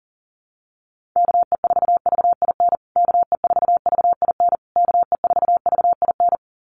Morse code, heard as a keyed tone, about 40 wpm, 700 Hz. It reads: KE4VIN KE4VIN KE4VIN
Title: USA Call Signs Repeated 3x 40wpm